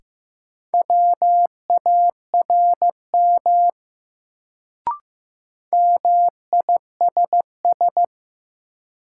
Morse code, heard as a keyed tone, about 15 wpm, 700 Hz. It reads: WARM MISS
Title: Sentences from Top 500 Words - New Word Drill - 15wpm